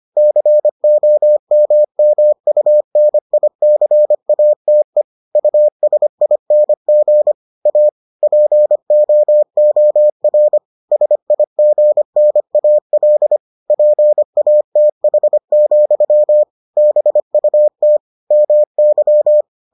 Morse code, heard as a keyed tone, about 25 wpm, 600 Hz. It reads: COMMUNICATE USING A POOR SIGNAL PATH, BUT MY